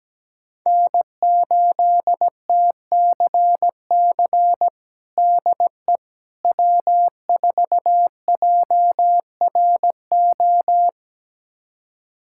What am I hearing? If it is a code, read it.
Morse code, heard as a keyed tone, about 17 wpm, 700 Hz. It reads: N8TCC DE W4JRO
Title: USA Callsign DE Callsign - 5 Characters - 17wpm